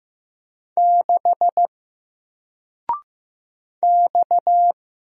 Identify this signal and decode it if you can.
Morse code, heard as a keyed tone, about 15 wpm, 700 Hz. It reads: 6 X
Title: Single Character (Letter:Number:Punctuation:Prosign) 15wpm